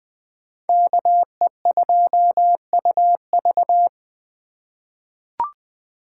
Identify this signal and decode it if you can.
Morse code, heard as a keyed tone, about 20 wpm, 700 Hz. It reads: KE2UV
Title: USA Call Signs - 5 Characters - 3x Character Spacing - 20wpm